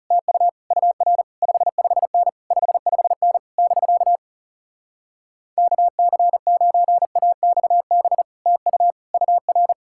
Morse code, heard as a keyed tone, about 40 wpm, 700 Hz. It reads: TU UR 55N 55N <BK> KC9AXB TU UR